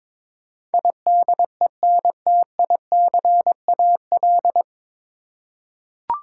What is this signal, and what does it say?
Morse code, heard as a keyed tone, about 22 wpm, 700 Hz. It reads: IDENTICAL
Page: Morse Code Ninja practice 22wpm